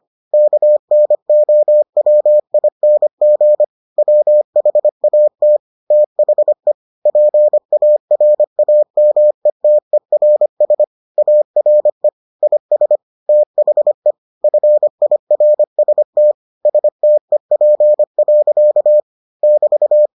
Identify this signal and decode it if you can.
Morse code, heard as a keyed tone, about 25 wpm, 600 Hz. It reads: KNOWING WHAT THE PARAMETERS ARE IS THE FIRST STEP. <BT>